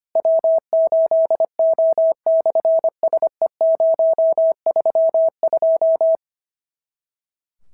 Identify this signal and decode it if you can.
Morse code, heard as a keyed tone, about 25 wpm, 650 Hz. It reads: W8O/SE032
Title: SOTA QSO Practice 25wpm